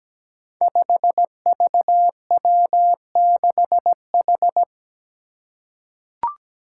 Morse code, heard as a keyed tone, about 17 wpm, 700 Hz. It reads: 5VW6H
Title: Five Challenging Letter-Number Combinations 17wpm